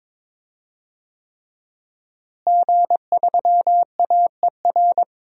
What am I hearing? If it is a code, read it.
Morse code, heard as a keyed tone, about 21 wpm, 700 Hz. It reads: G3AER
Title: DXCC-100 Call Signs - 5 Characters - Repeated 3x - 22wpm